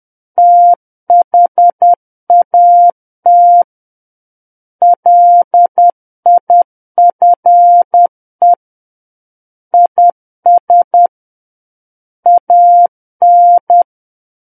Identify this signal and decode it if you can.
Morse code, heard as a keyed tone, about 10 wpm, 700 Hz. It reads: THAT LIFE IS AN
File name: Book One / 01-Chapter 1 - 10 WPM